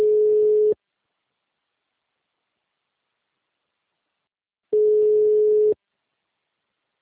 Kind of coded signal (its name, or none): telephone call progress tone